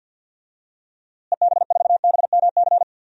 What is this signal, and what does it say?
Morse code, heard as a keyed tone, about 51 wpm, 700 Hz. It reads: EB4BMC
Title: DXCC-100 Call Signs - 6 Characters - Repeated 3x - 50wpm